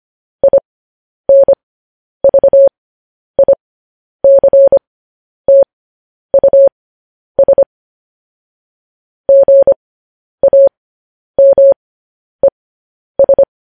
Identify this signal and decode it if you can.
Morse code, heard as a keyed tone, about 25 wpm, spaced as for 10 wpm, 550 Hz. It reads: INVICTUS GAMES